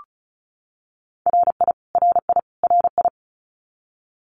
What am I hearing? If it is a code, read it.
Morse code, heard as a keyed tone, about 34 wpm, 700 Hz. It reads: RI RI RI
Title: US States Repeated 3x 35wpm